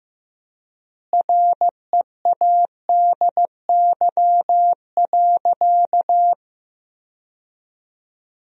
Morse code, heard as a keyed tone, about 15 wpm, 700 Hz. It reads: READY.
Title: 3x Word-Spacing - Sentences from Top 400 Words - 15wpm